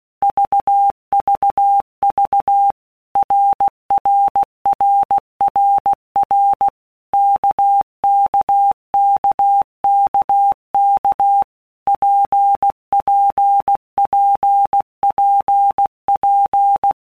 Morse code, heard as a keyed tone, about 16 wpm, 800 Hz. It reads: VVV RRRRR KKKKK PPPPP
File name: Basic Letter Groups By Blocks / Groups@15 / RKPX@15